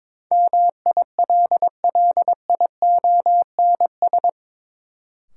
Morse code, heard as a keyed tone, about 22 wpm, 700 Hz. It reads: MILLIONS
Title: Sets of 5 Words - New Word Drill - 22wpm